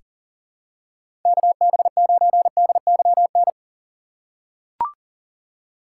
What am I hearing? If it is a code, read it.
Morse code, heard as a keyed tone, about 40 wpm, 700 Hz. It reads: KB9DYN
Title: USA Call Signs - 6 Characters - 3x Character Spacing - 40wpm